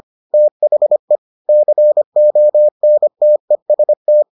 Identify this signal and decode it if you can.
Morse code, heard as a keyed tone, about 25 wpm, 600 Hz. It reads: THE CONTEST